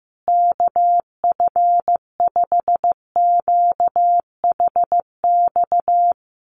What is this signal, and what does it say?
Morse code, heard as a keyed tone, about 15 wpm, 700 Hz. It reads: KF5QHX
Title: USA Call Signs Repeated 2x 15wpm